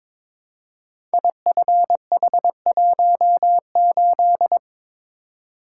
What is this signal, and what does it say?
Morse code, heard as a keyed tone, about 22 wpm, 690 Hz. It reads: IFH18
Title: Code Groups - 5 Letter:Numbers - 22wpm